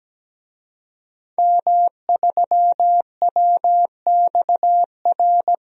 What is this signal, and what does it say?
Morse code, heard as a keyed tone, about 17 wpm, 700 Hz. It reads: M3WXR
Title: DXCC-100 Call Signs - 5 Characters - Repeated 3x - 17wpm